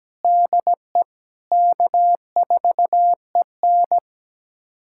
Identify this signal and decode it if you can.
Morse code, heard as a keyed tone, about 17 wpm, 700 Hz. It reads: DE K4EN
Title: USA Callsign DE Callsign - 4 Characters - 17wpm